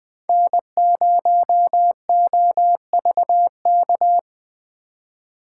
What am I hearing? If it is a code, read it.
Morse code, heard as a keyed tone, about 20 wpm, 700 Hz. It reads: N0OVK
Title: USA Call Signs - 4x Character Spacing - 20wpm